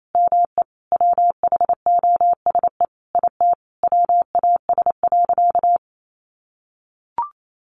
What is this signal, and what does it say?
Morse code, heard as a keyed tone, about 28 wpm, 700 Hz. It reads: ME WHOSE IT WAS.
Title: Sentences from Top 1200 Words 28wpm